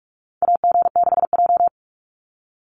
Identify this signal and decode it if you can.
Morse code, heard as a keyed tone, about 44 wpm, 700 Hz. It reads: AG6J
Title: USA Call Signs 45wpm